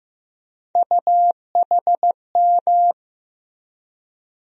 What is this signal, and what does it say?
Morse code, heard as a keyed tone, about 15 wpm, 700 Hz. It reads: UHM
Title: USA Call Sign Postfixes - 3 Letters 15wpm